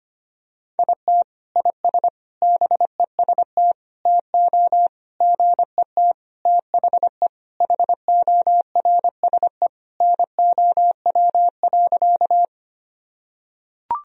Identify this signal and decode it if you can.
Morse code, heard as a keyed tone, about 25 wpm, 700 Hz. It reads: IT IS BEST TO GET THE HORSE NOW.
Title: Sentences from Top 1000 Words - Review Part 1 - 25wpm